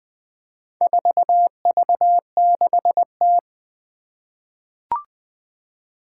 Morse code, heard as a keyed tone, about 20 wpm, 700 Hz. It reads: 4V6T
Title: Four Challenging Letter-Number Combinations 20wpm